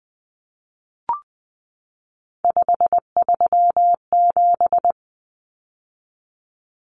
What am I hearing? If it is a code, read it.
Morse code, heard as a keyed tone, about 20 wpm, 700 Hz. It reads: 537
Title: Numbers - 3-Digit 20wpm